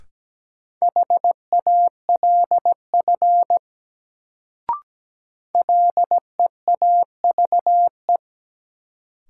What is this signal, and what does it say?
Morse code, heard as a keyed tone, about 17 wpm, 700 Hz. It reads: HALF LEAVE
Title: Sentences from Top 400 Words - New Word Drill - 17wpm